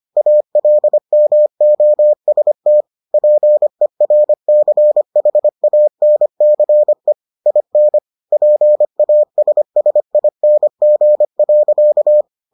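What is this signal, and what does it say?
Morse code, heard as a keyed tone, about 25 wpm, 600 Hz. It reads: ALMOST PERCHANCE IN PASSING.
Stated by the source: Morse edition of the Foundations of Amateur Radio episodes.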